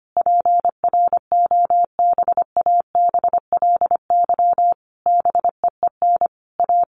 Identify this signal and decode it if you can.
Morse code, heard as a keyed tone, about 25 wpm, 700 Hz. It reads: PROBABLY BEEN A
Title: Sentences from Top 700 Words 25wpm